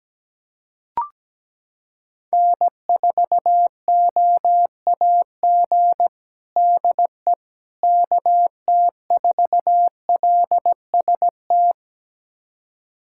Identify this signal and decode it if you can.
Morse code, heard as a keyed tone, about 17 wpm, 700 Hz. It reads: N4OAG DE KT4LST
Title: USA Callsign DE Callsign 17wpm